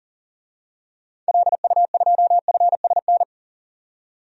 Morse code, heard as a keyed tone, about 40 wpm, 700 Hz. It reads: LU2FSN
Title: DXCC-100 Call Signs - 6 Characters - 40wpm